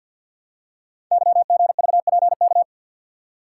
Morse code, heard as a keyed tone, about 50 wpm, 700 Hz. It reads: YGVPK
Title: Code Groups - 5 Letters - 50wpm